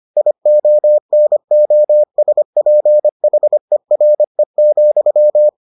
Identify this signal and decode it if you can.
Morse code, heard as a keyed tone, about 25 wpm, 600 Hz. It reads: IONOSPHERE,